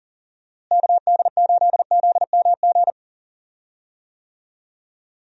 Morse code, heard as a keyed tone, about 37 wpm, 700 Hz. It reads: KD8ZMG
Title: USA Call Signs - 6 Characters - Repeated 2x - 40wpm